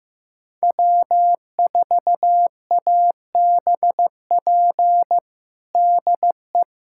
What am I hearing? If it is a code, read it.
Morse code, heard as a keyed tone, about 15 wpm, 700 Hz. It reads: W4ABP DE
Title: USA Callsign DE Callsign 15wpm